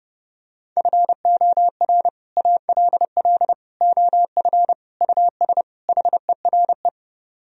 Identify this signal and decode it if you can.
Morse code, heard as a keyed tone, about 31 wpm, 700 Hz. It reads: FOR ALL OF US HERE